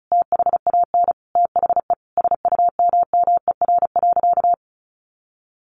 Morse code, heard as a keyed tone, about 34 wpm, 700 Hz. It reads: THAN THE SUMMER.
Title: Sentences from Top 1100 Words 35wpm